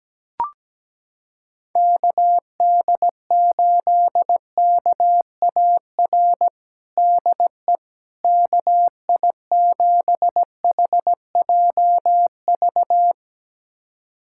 Morse code, heard as a keyed tone, about 17 wpm, 700 Hz. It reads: KD8KAR DE KI7HJV
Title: USA Callsign DE Callsign - 6 Characters - 17wpm